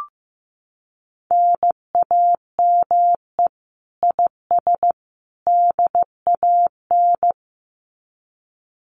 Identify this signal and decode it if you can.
Morse code, heard as a keyed tone, about 15 wpm, 700 Hz. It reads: NAME IS DAN
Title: Common QSO Phrases 15wpm